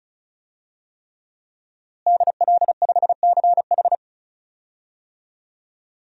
Morse code, heard as a keyed tone, about 36 wpm, 700 Hz. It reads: DL5CH
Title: DXCC-100 Call Signs - 5 Characters - Repeated 3x - 35wpm